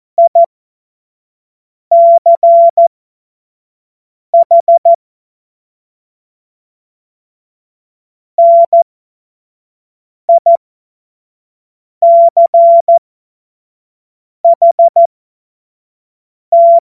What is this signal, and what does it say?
Morse code, heard as a keyed tone, about 14 wpm, spaced as for 5 wpm, 680 Hz. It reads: ICH NICHT
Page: cw-witze